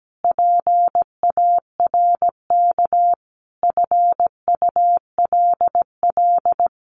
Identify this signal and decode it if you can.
Morse code, heard as a keyed tone, about 17 wpm, 700 Hz. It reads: PARK FULL
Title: Sentences from Top 1300 Words 17wpm